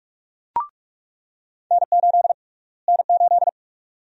Morse code, heard as a keyed tone, about 46 wpm, 690 Hz. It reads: N8 N8
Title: USA Call Sign Prefixes - Repeated 2x - 45wpm